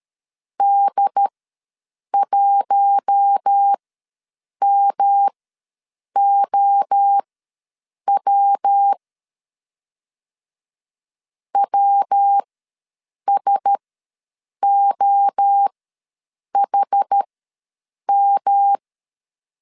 Morse code, heard as a keyed tone, about 13 wpm, spaced as for 7 wpm, 800 Hz. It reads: D1MOW WSOHM
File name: LEZIONI / 6 PKX0 / 6 P ws 12wpm 3117